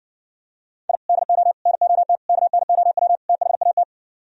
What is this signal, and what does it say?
Morse code, heard as a keyed tone, about 68 wpm, 700 Hz. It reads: I DO NOT KNOW THAT